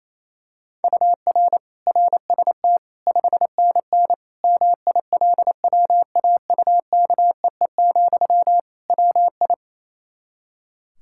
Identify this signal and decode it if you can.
Morse code, heard as a keyed tone, about 28 wpm, 700 Hz. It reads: UR RST 5NN MILWAUKEE, WI
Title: K1USN SST QSO Practice 28wpm